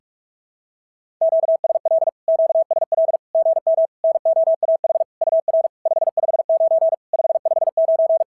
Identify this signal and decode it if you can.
Morse code, heard as a keyed tone, about 46 wpm, 650 Hz. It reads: QSL QSL GM NOAH UR 559 559
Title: SOTA QSO Practice 45wpm